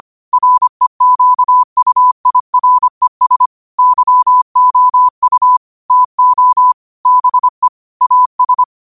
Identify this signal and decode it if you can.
Morse code, heard as a keyed tone, about 25 wpm, 1000 Hz. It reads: REQUIRES YOU TO BE AS
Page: New quotes every day in morse code at 25 Words per minute.